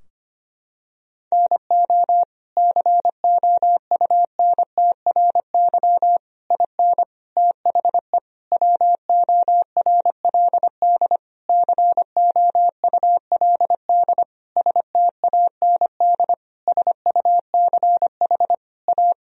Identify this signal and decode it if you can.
Morse code, heard as a keyed tone, about 25 wpm, 700 Hz. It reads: NO COUNTRY IN THE WORLD COULD STAND SUCH A
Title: Sentences from Top 1000 Words - Review Part 3 - 25wpm